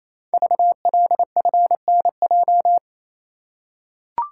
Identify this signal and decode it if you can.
Morse code, heard as a keyed tone, about 29 wpm, 690 Hz. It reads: VLFNJ
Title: Code Groups - 5 Letters - 28wpm